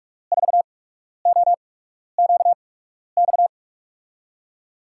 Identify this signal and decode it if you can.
Morse code, heard as a keyed tone, about 44 wpm, spaced as for 12 wpm, 700 Hz. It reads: 4OQX
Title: USA Call Signs - 8x Character Spacing - 45wpm